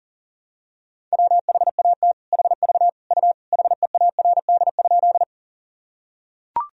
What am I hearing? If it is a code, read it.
Morse code, heard as a keyed tone, about 40 wpm, 700 Hz. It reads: WHAT HV U HEARD?
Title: Sentences from Top 400 Words and Abbreviations 40wpm